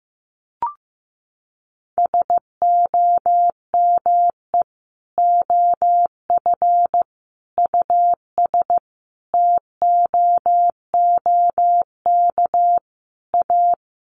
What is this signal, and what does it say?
Morse code, heard as a keyed tone, about 14 wpm, 700 Hz. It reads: SOME OF US TOOK A
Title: Sentences from Top 1400 Words 15wpm